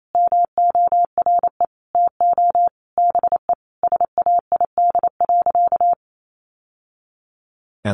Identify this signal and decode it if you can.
Morse code, heard as a keyed tone, about 28 wpm, 700 Hz. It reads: MORE TO BE SAID.
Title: Sentences from Top 1000 Words 28wpm